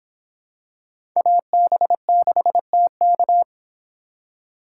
Morse code, heard as a keyed tone, about 25 wpm, 700 Hz. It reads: AB6TK